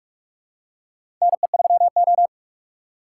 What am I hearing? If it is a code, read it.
Morse code, heard as a keyed tone, about 46 wpm, 700 Hz. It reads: NE3O
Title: USA Call Signs - 4 Characters - 3x Character Spacing - 45wpm